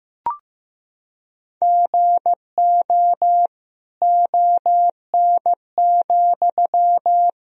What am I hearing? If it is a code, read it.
Morse code, heard as a keyed tone, about 15 wpm, 700 Hz. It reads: GO ON,
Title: Sentences from Top 1000 Words - Review Part 4 - 15wpm